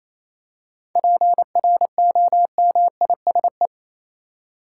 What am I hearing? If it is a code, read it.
Morse code, heard as a keyed tone, about 28 wpm, 700 Hz. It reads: PROMISE
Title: Seven Letter Words - 28wpm